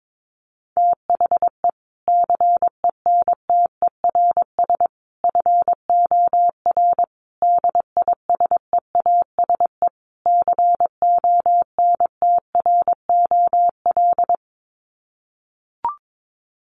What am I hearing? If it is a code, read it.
Morse code, heard as a keyed tone, about 22 wpm, 700 Hz. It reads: THE CENTERS FOR DISEASE CONTROL
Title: Sets of 5 Words Encore2 22wpm